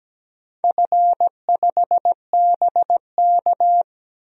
Morse code, heard as a keyed tone, about 17 wpm, 700 Hz. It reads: F5BK
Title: DXCC-100 Call Signs - 4 Characters - 17wpm